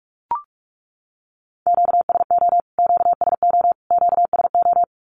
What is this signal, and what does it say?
Morse code, heard as a keyed tone, about 44 wpm, 700 Hz. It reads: QSO QSO QSO
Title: Q-Codes Repeated 3x 45wpm